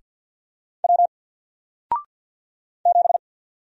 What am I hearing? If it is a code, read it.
Morse code, heard as a keyed tone, about 48 wpm, 700 Hz. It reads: W 7
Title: Single Character (Letter:Number:Punctuation:Prosign) Alt 50wpm